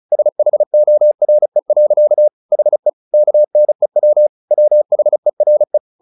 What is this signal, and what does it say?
Morse code, heard as a keyed tone, about 34 wpm, 600 Hz. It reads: SHORE. HE KNEW WHERE
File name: the old man and the sea 2 35wpm